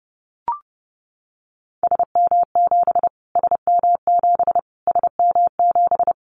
Morse code, heard as a keyed tone, about 30 wpm, 700 Hz. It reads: SM7 SM7 SM7
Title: Callsign Prefixes for Top 25 DXCC - Repeated 3x - 30wpm